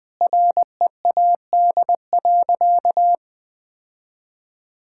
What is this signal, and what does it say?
Morse code, heard as a keyed tone, about 20 wpm, 700 Hz. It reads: READ.
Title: Sentences with 4-Word Phrases - 20wpm